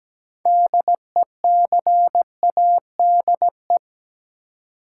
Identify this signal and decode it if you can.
Morse code, heard as a keyed tone, about 17 wpm, 700 Hz. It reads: DECADE
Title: Sets of 5 Words - New Word Drill - 17wpm